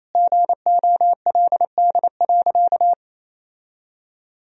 Morse code, heard as a keyed tone, about 27 wpm, 700 Hz. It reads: GOLD.
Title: Sentences from Top 1000 Words - Review Part 3 - 28wpm